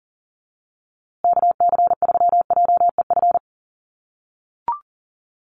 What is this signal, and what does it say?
Morse code, heard as a keyed tone, about 40 wpm, 700 Hz. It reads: KC3JEF
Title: USA Call Signs Repeated 3x 40wpm